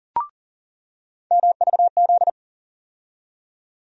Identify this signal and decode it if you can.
Morse code, heard as a keyed tone, about 40 wpm, 700 Hz. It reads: MVZ
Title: USA Call Sign Postfixes - 3 Letters 40wpm